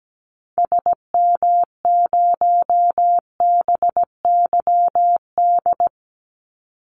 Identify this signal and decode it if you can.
Morse code, heard as a keyed tone, about 17 wpm, 700 Hz. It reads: SM0BYD
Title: Call Signs Repeated 2x - 17wpm